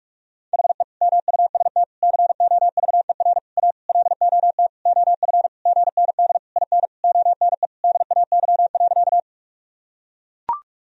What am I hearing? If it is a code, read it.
Morse code, heard as a keyed tone, about 41 wpm, 700 Hz. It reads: HE MUST COVER A LOT OF GND IN ONE DAY.
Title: Sentences from Top 500 Words and Abbreviations 45wpm